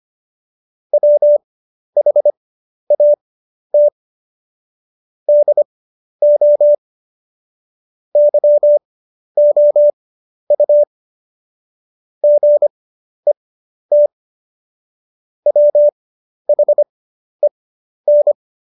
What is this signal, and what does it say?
Morse code, heard as a keyed tone, about 25 wpm, spaced as for 11 wpm, 590 Hz. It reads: WHAT DO YOU GET WHEN
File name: Jokes-25-char-10-eff